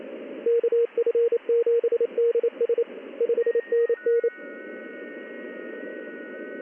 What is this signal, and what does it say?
Morse code, heard as a keyed tone, about 28 wpm, 470 Hz. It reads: KF7DS 5NN
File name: contests / ss-ssb-k5tr- -1700